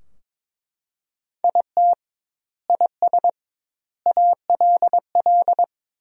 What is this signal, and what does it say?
Morse code, heard as a keyed tone, about 23 wpm, 700 Hz. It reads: IT IS ALL
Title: 2x Word-Spacing - Sentences from Top 500 Words - 22wpm